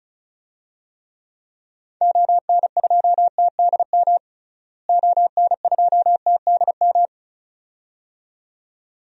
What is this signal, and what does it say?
Morse code, heard as a keyed tone, about 36 wpm, 700 Hz. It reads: ON2TDM ON2TDM
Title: DXCC-100 Call Signs - 6 Characters - Repeated 2x - 35wpm